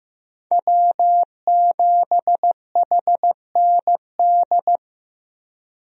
Morse code, heard as a keyed tone, about 15 wpm, 700 Hz. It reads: W7HND
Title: USA Call Signs - 5 Characters - Repeated 2x - 15wpm